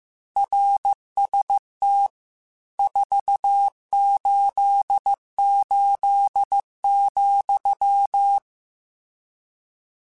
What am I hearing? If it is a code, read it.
Morse code, heard as a keyed tone, about 15 wpm, 800 Hz. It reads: RST 488,
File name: RSTs / RST-mp3s / Your RSTs @13